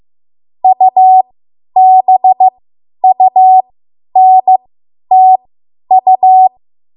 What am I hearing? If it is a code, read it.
Morse code, heard as a keyed tone, about 15 wpm, spaced as for 10 wpm, 750 Hz. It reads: UBUNTU
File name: Morse / 10WPM